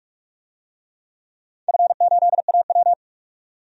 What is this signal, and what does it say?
Morse code, heard as a keyed tone, about 44 wpm, 700 Hz. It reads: F8AW
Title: DXCC-100 Call Signs - 4 Characters - 45wpm